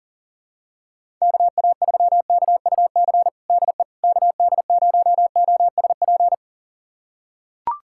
Morse code, heard as a keyed tone, about 39 wpm, 700 Hz. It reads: KA3KUC DE KD0OSP
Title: USA Callsign DE Callsign 40wpm